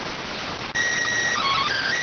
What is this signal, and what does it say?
SSTV (slow-scan television), PD 290.